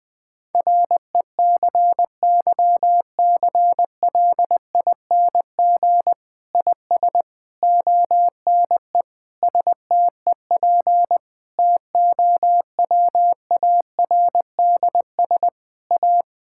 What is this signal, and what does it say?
Morse code, heard as a keyed tone, about 20 wpm, 700 Hz. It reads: RECYCLING IS ONE STEP TOWARDS A
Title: Difficult Sentences with Hints - 20wpm